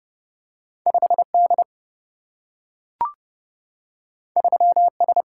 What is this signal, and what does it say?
Morse code, heard as a keyed tone, about 30 wpm, 700 Hz. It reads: 5 D 3 S
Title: Single-Letter Single-Number 30wpm